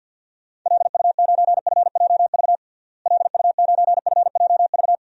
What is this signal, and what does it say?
Morse code, heard as a keyed tone, about 46 wpm, 700 Hz. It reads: LU9FJV LU9FJV
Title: DXCC-100 Call Signs - 6 Characters - Repeated 3x - 50wpm